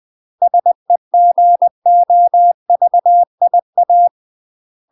Morse code, heard as a keyed tone, about 20 wpm, 700 Hz. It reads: SEGOVIA
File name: UK MASTER QSOs TRAINING/20WPM